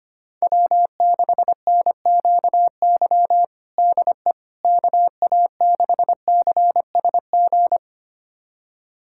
Morse code, heard as a keyed tone, about 25 wpm, 700 Hz. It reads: W6NQY DE KA6CSG
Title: USA Callsign DE Callsign 25wpm